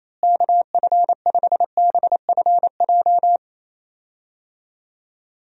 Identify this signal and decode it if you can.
Morse code, heard as a keyed tone, about 28 wpm, 700 Hz. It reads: KF5BFJ
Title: USA Call Signs - 6 Characters - Repeated 3x - 28wpm